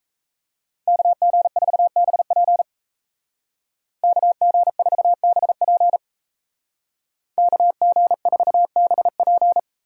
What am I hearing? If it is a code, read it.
Morse code, heard as a keyed tone, about 36 wpm, 700 Hz. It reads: KG4BP KG4BP KG4BP